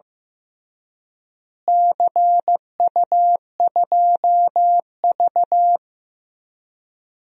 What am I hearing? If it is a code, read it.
Morse code, heard as a keyed tone, about 15 wpm, 700 Hz. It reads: CU2V
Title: DXCC-100 Call Signs - 4 Characters - Repeated 2x - 15wpm